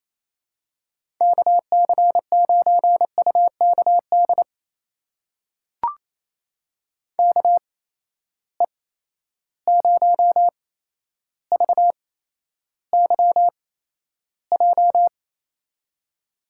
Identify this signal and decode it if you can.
Morse code, heard as a keyed tone, about 28 wpm, spaced as for 12 wpm, 700 Hz. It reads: KC9UKD KE0VYJ
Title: USA Call Signs - 6 Characters - 8x Character Spacing - 28wpm